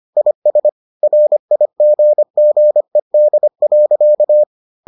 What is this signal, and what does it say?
Morse code, heard as a keyed tone, about 25 wpm, 600 Hz. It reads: IS RIGGED.